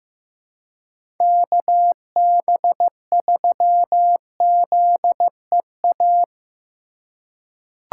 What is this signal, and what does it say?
Morse code, heard as a keyed tone, about 15 wpm, 700 Hz. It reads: KB3ZEA
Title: USA Call Signs - 2x Character Spacing - 15wpm